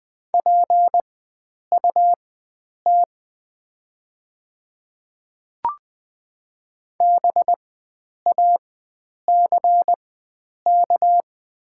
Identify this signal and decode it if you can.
Morse code, heard as a keyed tone, about 20 wpm, spaced as for 9 wpm, 700 Hz. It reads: PUT BACK
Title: 4x Character Spacing - Top 500 Words - 20wpm